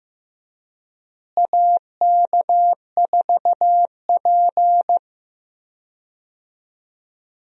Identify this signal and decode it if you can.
Morse code, heard as a keyed tone, about 15 wpm, 700 Hz. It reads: AK4P
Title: USA Call Signs - 4 Characters - Repeated 2x - 15wpm